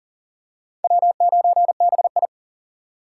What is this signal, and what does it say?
Morse code, heard as a keyed tone, about 37 wpm, 700 Hz. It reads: W9BI
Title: USA Call Signs - 4 Characters - 2x Character Spacing - 40wpm